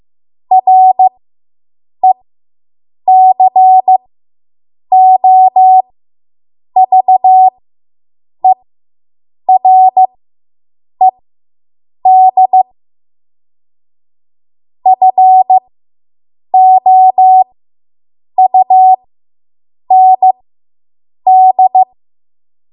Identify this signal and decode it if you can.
Morse code, heard as a keyed tone, about 15 wpm, spaced as for 7 wpm, 750 Hz. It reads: RECOVERED FOUND